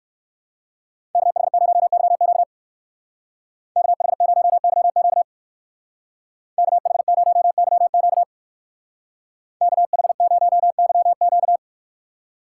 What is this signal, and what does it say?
Morse code, heard as a keyed tone, about 58 wpm, 700 Hz. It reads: KH0YQ KH0YQ KH0YQ KH0YQ